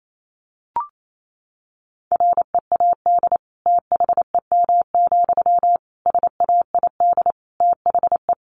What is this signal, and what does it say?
Morse code, heard as a keyed tone, about 28 wpm, 700 Hz. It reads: READ THEM, SAID THE
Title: Sentences from Top 400 Words 28wpm